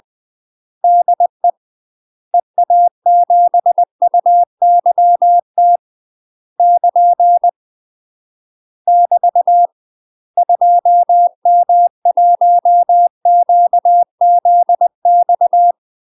Morse code, heard as a keyed tone, about 20 wpm, 700 Hz. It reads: DE EA7UYT <KN> <BT> 2M1QZX
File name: UK MASTER QSOs TRAINING/20WPM